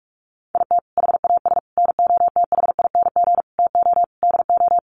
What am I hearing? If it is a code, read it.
Morse code, heard as a keyed tone, about 44 wpm, 700 Hz. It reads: IT HAS NOTHING TO DO
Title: Sets of 5 Words Encore 45wpm